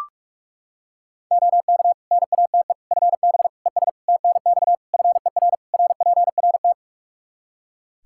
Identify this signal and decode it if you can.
Morse code, heard as a keyed tone, about 42 wpm, 700 Hz. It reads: OK NATE FB ES TNX FER RPRT
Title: Practice QSO - Exchange 2 - Info - 45wpm